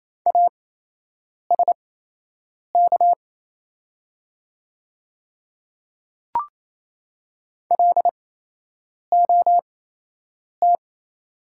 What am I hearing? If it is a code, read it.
Morse code, heard as a keyed tone, about 28 wpm, spaced as for 8 wpm, 690 Hz. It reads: ASK LOT
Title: 8x Character Spacing - Top 500 Words - 28wpm